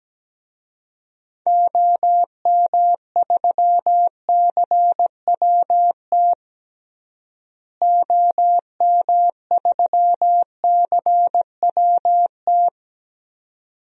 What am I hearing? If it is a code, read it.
Morse code, heard as a keyed tone, about 17 wpm, 700 Hz. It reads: OM3CWT OM3CWT
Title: DXCC-100 Call Signs - 6 Characters - Repeated 2x - 17wpm